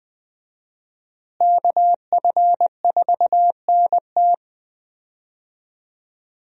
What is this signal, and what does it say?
Morse code, heard as a keyed tone, about 20 wpm, 700 Hz. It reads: KF4NT
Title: USA Call Signs - 5 Characters - Repeated 3x - 20wpm